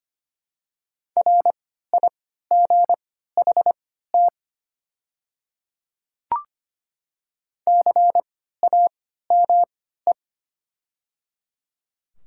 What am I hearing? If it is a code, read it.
Morse code, heard as a keyed tone, about 25 wpm, spaced as for 14 wpm, 700 Hz. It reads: RIGHT CAME
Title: 3x Character Spacing - Top 200 Words - 25wpm